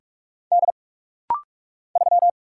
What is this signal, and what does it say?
Morse code, heard as a keyed tone, about 44 wpm, 700 Hz. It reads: D 3
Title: Single Letter-Number Alt - Rapid-Fire 45wpm